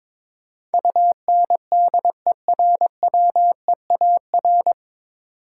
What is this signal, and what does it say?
Morse code, heard as a keyed tone, about 21 wpm, 700 Hz. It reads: UNDERWEAR
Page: Morse Code Ninja practice 22wpm